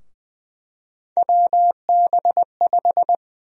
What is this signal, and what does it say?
Morse code, heard as a keyed tone, about 20 wpm, 700 Hz. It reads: WB5
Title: USA Call Sign Prefixes - 20wpm